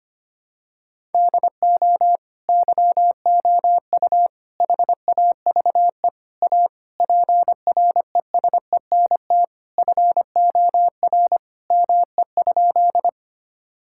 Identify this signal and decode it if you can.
Morse code, heard as a keyed tone, about 25 wpm, 700 Hz. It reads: DO YOU HAVE A PRESENT FOR ME?
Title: Sentences from Top 1000 Words - Review Part 5 - 25wpm